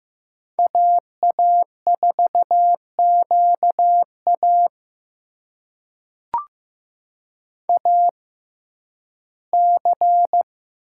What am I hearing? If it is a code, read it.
Morse code, heard as a keyed tone, about 15 wpm, spaced as for 11 wpm, 700 Hz. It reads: AA4QA AC
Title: USA Call Signs - 5 Characters - 6x Character Spacing - 15wpm